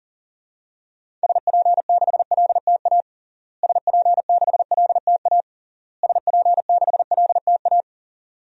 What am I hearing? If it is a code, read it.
Morse code, heard as a keyed tone, about 40 wpm, 700 Hz. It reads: SP6LTA SP6LTA SP6LTA
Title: DXCC-100 Call Signs - 6 Characters - Repeated 3x - 40wpm